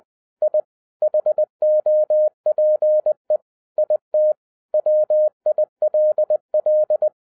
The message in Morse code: I HOPE IT WILL

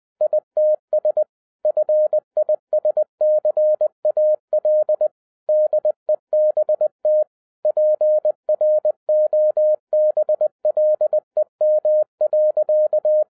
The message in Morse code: ITS FISCAL DEBT PROBLEM.

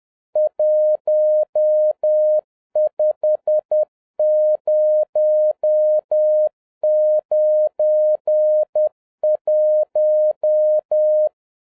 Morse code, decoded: 15091

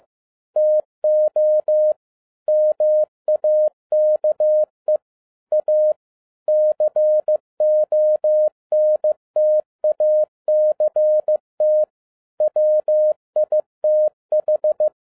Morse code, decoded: TO MAKE A CONTACT WITH

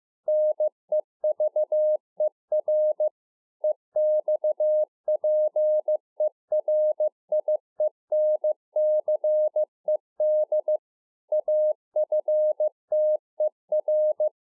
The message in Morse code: NEVER EXPERIENCED AFTER

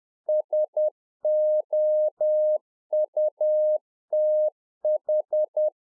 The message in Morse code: SOUTH

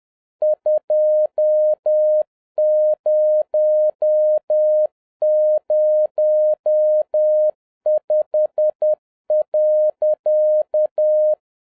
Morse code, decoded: 2005.

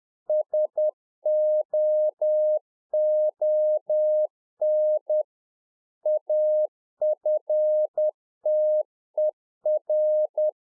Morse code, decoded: SOON AFTER

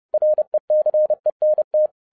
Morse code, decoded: RECENT

